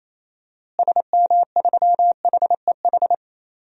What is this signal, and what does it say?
Morse code, heard as a keyed tone, about 28 wpm, 700 Hz. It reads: SM3HEH
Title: DXCC-100 Call Signs - 6 Characters - Repeated 2x - 28wpm